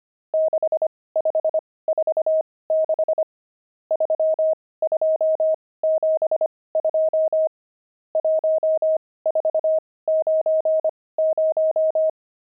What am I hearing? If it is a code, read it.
Morse code, heard as a keyed tone, about 25 wpm, spaced as for 18 wpm, 630 Hz. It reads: 6546 3272 1490
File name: pangram-fs-18-cs-25-nums8